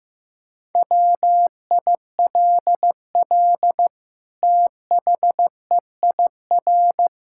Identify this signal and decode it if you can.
Morse code, heard as a keyed tone, about 15 wpm, 700 Hz. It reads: WILL THEIR
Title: Sentences from Top 1000 Words - Review Part 5 - 15wpm